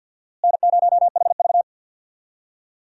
Morse code, heard as a keyed tone, about 46 wpm, 700 Hz. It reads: N0HV
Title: USA Call Signs - 4 Characters - Repeated 3x - 50wpm